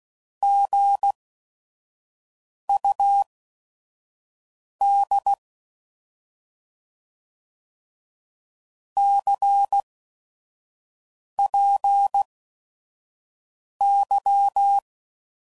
Morse code, heard as a keyed tone, about 16 wpm, spaced as for 5 wpm, 800 Hz. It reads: GUD CPY
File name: Mixed / Misc / Words